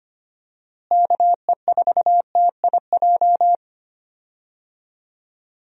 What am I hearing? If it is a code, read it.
Morse code, heard as a keyed tone, about 25 wpm, 700 Hz. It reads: KE4TIJ
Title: USA Call Signs - 6 Characters - Repeated 2x - 25wpm